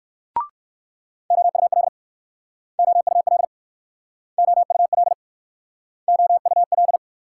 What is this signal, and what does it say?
Morse code, heard as a keyed tone, about 63 wpm, 700 Hz. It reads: OUL OUL OUL OUL